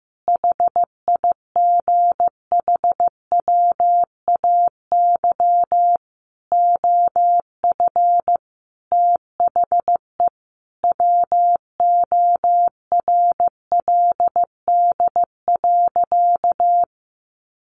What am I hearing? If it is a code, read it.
Morse code, heard as a keyed tone, about 15 wpm, 700 Hz. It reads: HIGHWAY OF THE WORLD.
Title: Sentences from Top 1400 Words 15wpm